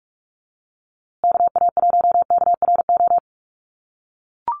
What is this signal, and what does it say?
Morse code, heard as a keyed tone, about 44 wpm, 700 Hz. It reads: KA1KRO
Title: USA Call Signs Repeated 2x 45wpm